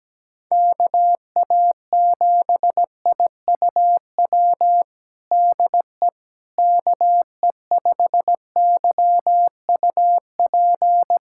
Morse code, heard as a keyed tone, about 17 wpm, 700 Hz. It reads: KA7IUW DE KE5YUP
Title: USA Callsign DE Callsign - 6 Characters - 17wpm